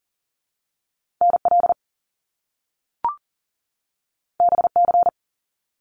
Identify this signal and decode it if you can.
Morse code, heard as a keyed tone, about 39 wpm, 700 Hz. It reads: NL BC
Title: Canadian Provinces and Territories 40wpm